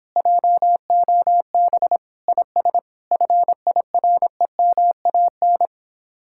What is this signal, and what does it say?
Morse code, heard as a keyed tone, about 26 wpm, 700 Hz. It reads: JOB IS FIREMAN